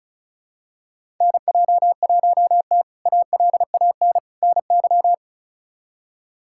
Morse code, heard as a keyed tone, about 36 wpm, 700 Hz. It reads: NJ1T ALAN NY
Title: K1USN SST QSO Practice - 35wpm